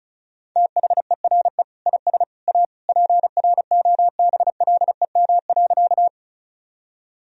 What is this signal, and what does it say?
Morse code, heard as a keyed tone, about 34 wpm, 700 Hz. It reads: THERE IS A PROBLEM.
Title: Sentences from Top 1000 Words - Review Part 2 - 35wpm